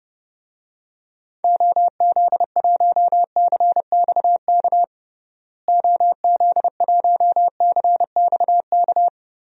Morse code, heard as a keyed tone, about 30 wpm, 700 Hz. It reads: OZ1CXK OZ1CXK
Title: DXCC-100 Call Signs - 6 Characters - Repeated 3x - 30wpm